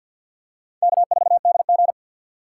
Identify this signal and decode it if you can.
Morse code, heard as a keyed tone, about 50 wpm, 700 Hz. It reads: K4DG
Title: USA Call Signs - 4 Characters - 3x Character Spacing - 50wpm